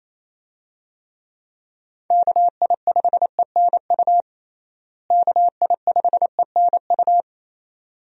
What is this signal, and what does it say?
Morse code, heard as a keyed tone, about 28 wpm, 700 Hz. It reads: KI5ENU KI5ENU
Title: USA Call Signs - 6 Characters - Repeated 2x - 28wpm